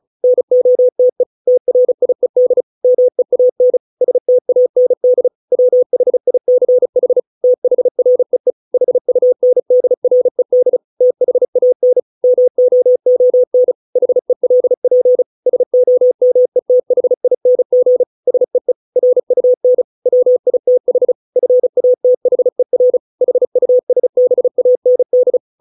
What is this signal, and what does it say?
Morse code, heard as a keyed tone, about 34 wpm, 500 Hz. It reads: NOTE TRIED MEAN STAND WHICH THREE HUNDRED THAN MOON HELP SOMETHING SEE RUN WITH FATHER HUSBAND